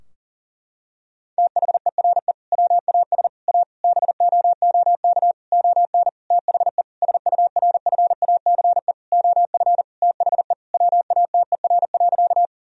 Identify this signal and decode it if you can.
Morse code, heard as a keyed tone, about 40 wpm, 700 Hz. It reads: THERE WAS A BOOK ON THE SURFACE OF THE WATER.
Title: Sentences from Top 1000 Words - Review Part 3 - 40wpm